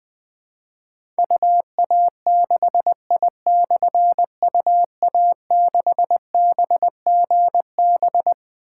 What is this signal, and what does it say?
Morse code, heard as a keyed tone, about 20 wpm, 700 Hz. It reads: UA6I/UA6BGB
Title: Difficult Call Signs 20wpm